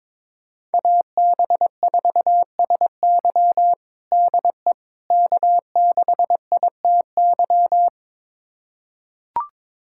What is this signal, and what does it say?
Morse code, heard as a keyed tone, about 22 wpm, 700 Hz. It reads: AB4SY DE K6ITY
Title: USA Callsign DE Callsign - 5 Characters - 22wpm